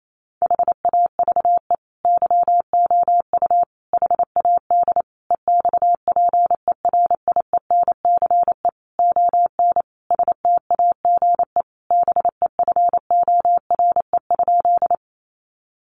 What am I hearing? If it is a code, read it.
Morse code, heard as a keyed tone, about 28 wpm, 700 Hz. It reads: HAVE YOU HAD EXPERIENCE ON STAGE BEFORE?
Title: Sentences from Top 1200 Words 28wpm